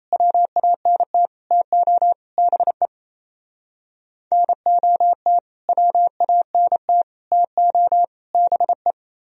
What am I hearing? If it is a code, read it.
Morse code, heard as a keyed tone, about 35 wpm, 700 Hz. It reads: WANT TO BE NOT WANT TO BE
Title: Speed-Racing - Sets of 4 Words 28wpm